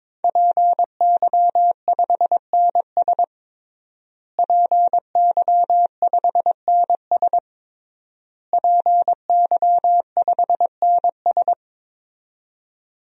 Morse code, heard as a keyed tone, about 22 wpm, 700 Hz. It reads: PY5NS PY5NS PY5NS
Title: DXCC-100 Call Signs - 5 Characters - Repeated 3x - 22wpm